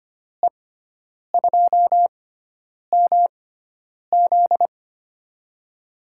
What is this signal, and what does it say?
Morse code, heard as a keyed tone, about 25 wpm, spaced as for 9 wpm, 700 Hz. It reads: E2MZ
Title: USA Call Signs - 5 Characters - 6x Character Spacing - 25wpm